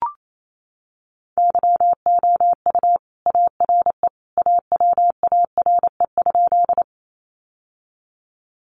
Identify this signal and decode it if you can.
Morse code, heard as a keyed tone, about 28 wpm, 700 Hz. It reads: YOU ARE AWARE?
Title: Sentences from Top 1200 Words 28wpm